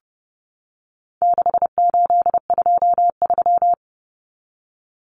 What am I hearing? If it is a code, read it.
Morse code, heard as a keyed tone, about 30 wpm, 700 Hz. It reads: 6823
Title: Numbers - 4-Digit 30wpm